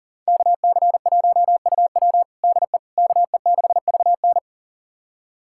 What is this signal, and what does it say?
Morse code, heard as a keyed tone, about 40 wpm, 700 Hz. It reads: KC1UW DE KE6VN
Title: USA Callsign DE Callsign - 5 Characters - 40wpm